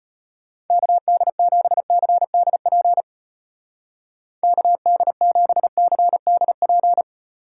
Morse code, heard as a keyed tone, about 32 wpm, 700 Hz. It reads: KD7CDP KD7CDP